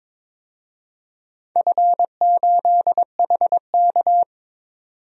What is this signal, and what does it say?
Morse code, heard as a keyed tone, about 22 wpm, 700 Hz. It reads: F8HK
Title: DXCC-100 Call Signs - 4 Characters - 22wpm